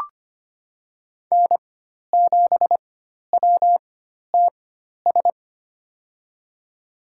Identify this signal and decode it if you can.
Morse code, heard as a keyed tone, about 25 wpm, spaced as for 12 wpm, 700 Hz. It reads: N7WTS
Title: USA Call Signs - 5 Characters - 4x Character Spacing - 25wpm